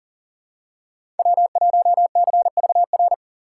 Morse code, heard as a keyed tone, about 40 wpm, 700 Hz. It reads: W1CVR
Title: USA Call Signs - 5 Characters - 3x Character Spacing - 40wpm